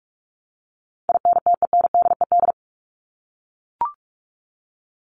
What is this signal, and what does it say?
Morse code, heard as a keyed tone, about 44 wpm, 700 Hz. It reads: INTENDED
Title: ED Suffix - Top 100 Words with Base Words 45wpm